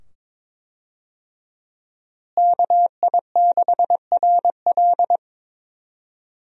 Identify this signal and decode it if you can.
Morse code, heard as a keyed tone, about 21 wpm, 700 Hz. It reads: KI6RL
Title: USA Call Signs - 5 Characters - Repeated 2x - 22wpm